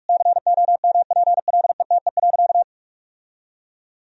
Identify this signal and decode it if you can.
Morse code, heard as a keyed tone, about 41 wpm, 700 Hz. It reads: COMPLETE.
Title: Sentences from Top 1000 Words - Review Part 2 - 45wpm